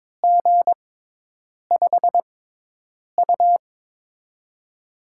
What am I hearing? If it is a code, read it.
Morse code, heard as a keyed tone, about 23 wpm, 700 Hz. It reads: G 5 U
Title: USA Call Signs - 6x Character Spacing - 22wpm